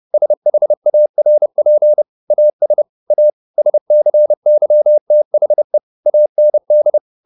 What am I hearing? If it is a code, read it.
Morse code, heard as a keyed tone, about 30 wpm, 600 Hz. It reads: SHARP AS A SCYTHE AND
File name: the old man and the sea / the old man and the sea 1 30wpm